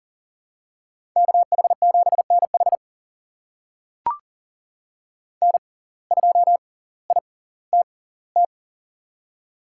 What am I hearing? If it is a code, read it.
Morse code, heard as a keyed tone, about 39 wpm, spaced as for 20 wpm, 700 Hz. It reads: KH7NH N2ITT
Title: USA Call Signs - 5 Characters - 6x Character Spacing - 40wpm